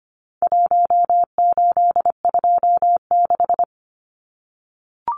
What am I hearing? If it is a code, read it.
Morse code, heard as a keyed tone, about 25 wpm, 700 Hz. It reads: 1826
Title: Numbers - Contest Style - 4-Digit 25wpm